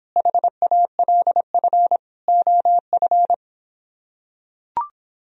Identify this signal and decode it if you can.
Morse code, heard as a keyed tone, about 26 wpm, 700 Hz. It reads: HALF OF